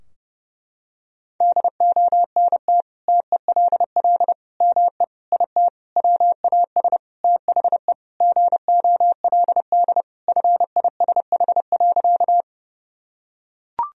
Morse code, heard as a keyed tone, about 30 wpm, 700 Hz. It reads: DONT TELL ME IT WAS THE GOLD FISH.
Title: Sentences from Top 1000 Words - Review Part 3 - 30wpm